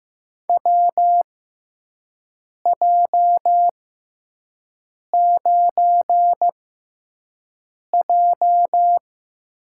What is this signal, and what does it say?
Morse code, heard as a keyed tone, about 15 wpm, spaced as for 5 wpm, 700 Hz. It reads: WJ9J
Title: USA Call Signs - 4 Characters - 6x Character Spacing - 15wpm